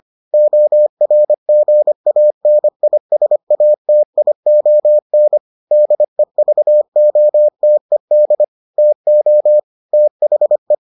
Morse code, heard as a keyed tone, about 25 wpm, 600 Hz. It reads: ORGANISATION DEVOTED TO THE